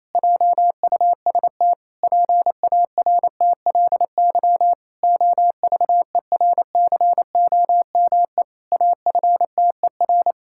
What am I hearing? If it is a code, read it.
Morse code, heard as a keyed tone, about 28 wpm, 700 Hz. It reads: JUST PARTLY OVERCOME AFTER
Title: Pangram Challenge v2 - 28wpm